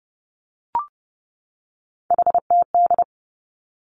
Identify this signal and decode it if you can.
Morse code, heard as a keyed tone, about 30 wpm, 700 Hz. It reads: HTD
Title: Three Challenging Letter-Number Combinations 30wpm